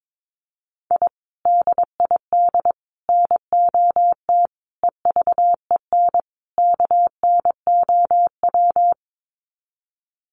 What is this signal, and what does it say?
Morse code, heard as a keyed tone, about 22 wpm, 700 Hz. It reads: I DID NOT EVEN KNOW
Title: Sets of 5 Words Encore 22wpm